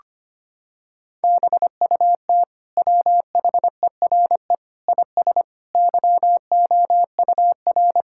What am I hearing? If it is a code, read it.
Morse code, heard as a keyed tone, about 25 wpm, 700 Hz. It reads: BUT WHERE IS YOUR
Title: Sentences from Top 1000 Words - Review Part 5 - 25wpm